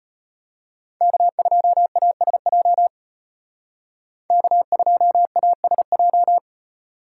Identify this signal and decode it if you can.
Morse code, heard as a keyed tone, about 32 wpm, 700 Hz. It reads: K2ASJ K2ASJ